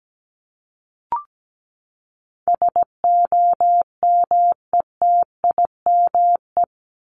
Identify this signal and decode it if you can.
Morse code, heard as a keyed tone, about 17 wpm, 700 Hz. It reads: SOMETIME
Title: ES:S Suffix - Top 100 Words with Base Words 17wpm